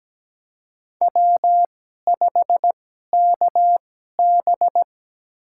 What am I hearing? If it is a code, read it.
Morse code, heard as a keyed tone, about 17 wpm, spaced as for 12 wpm, 700 Hz. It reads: W5KB
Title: USA Call Signs - 4 Characters - 2x Character Spacing - 17wpm